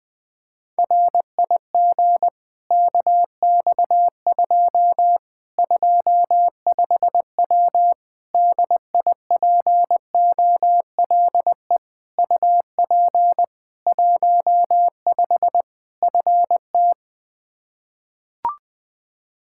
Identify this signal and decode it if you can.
Morse code, heard as a keyed tone, about 20 wpm, 700 Hz. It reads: RIG KX2 25W DIPOLE UP 15 FT
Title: Practice QSO - Exchange 2 - Info - 20wpm